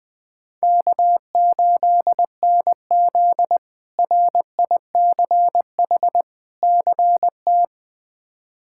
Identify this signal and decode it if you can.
Morse code, heard as a keyed tone, about 20 wpm, 700 Hz. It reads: K8NZ RICH CT
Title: K1USN SST QSO Practice - 20wpm